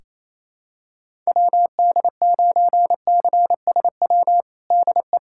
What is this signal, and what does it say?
Morse code, heard as a keyed tone, about 28 wpm, 700 Hz. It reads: WD9CSW DE
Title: USA Callsign DE Callsign 28wpm